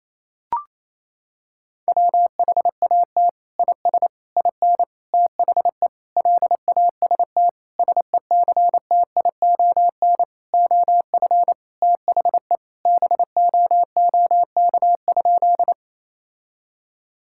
Morse code, heard as a keyed tone, about 28 wpm, 700 Hz. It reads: WHAT IS IN THE LAST SECTION OF THE BOOK?
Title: Sentences from Top 1000 Words - Review Part 5 - 28wpm